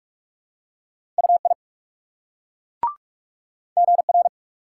Morse code, heard as a keyed tone, about 38 wpm, 700 Hz. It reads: UI GR